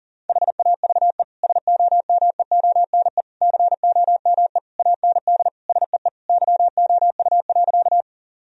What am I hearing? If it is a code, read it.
Morse code, heard as a keyed tone, about 40 wpm, 700 Hz. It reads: HAVE SOMEONE COME AND SEE YOU.
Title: Sentences from Top 1000 Words - Review Part 4 - 40wpm